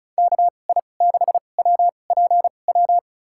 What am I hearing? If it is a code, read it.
Morse code, heard as a keyed tone, about 36 wpm, spaced as for 25 wpm, 700 Hz. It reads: KI6WPW
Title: USA Call Signs - 2x Character Spacing - 35wpm